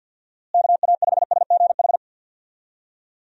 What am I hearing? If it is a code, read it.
Morse code, heard as a keyed tone, about 49 wpm, 700 Hz. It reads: KA5SGH
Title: USA Call Signs - 6 Characters - Repeated 2x - 50wpm